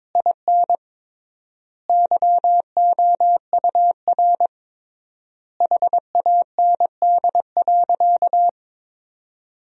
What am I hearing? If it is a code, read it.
Morse code, heard as a keyed tone, about 22 wpm, 700 Hz. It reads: IN YOUR HAND.
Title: 3x Word-Spacing - Sentences from Top 400 Words - 22wpm